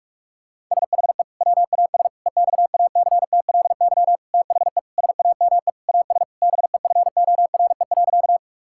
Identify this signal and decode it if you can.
Morse code, heard as a keyed tone, about 46 wpm, 700 Hz. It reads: SHE WAS EXACTLY THE SAME AS BEFORE.
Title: Sentences from Top 1000 Words - Review Part 6 - 45wpm